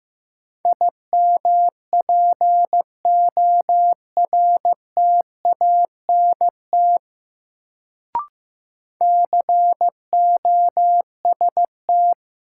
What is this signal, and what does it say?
Morse code, heard as a keyed tone, about 15 wpm, 700 Hz. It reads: IMPORTANT COST
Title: Sets of 3 Words - New Word Drill - 15wpm